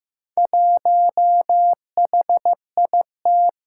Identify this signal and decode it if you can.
Morse code, heard as a keyed tone, about 15 wpm, 700 Hz. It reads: 1HIT
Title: Four Challenging Letter-Number Combinations 15wpm